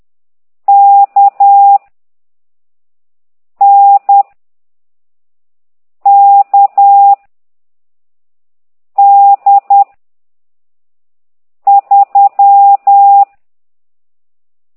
Morse code, heard as a keyed tone, about 10 wpm, 800 Hz. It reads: K N K D 3
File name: ct2idw / exerc13